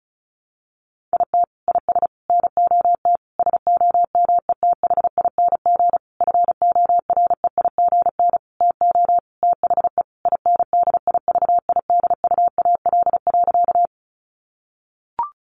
Morse code, heard as a keyed tone, about 34 wpm, 700 Hz. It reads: IT IS NOT SOMETHING FOREIGN TO THE INDIVIDUAL.
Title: Sentences from Top 1400 Words 35wpm